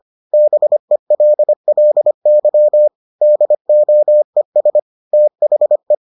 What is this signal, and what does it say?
Morse code, heard as a keyed tone, about 25 wpm, 600 Hz. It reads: BELLY DOES THE